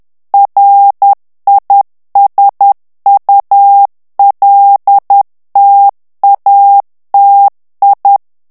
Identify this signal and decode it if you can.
Morse code, heard as a keyed tone, about 10 wpm, 800 Hz. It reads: RISULTATI
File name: ITALIEN texte clair 2 vitesse 10 mots minute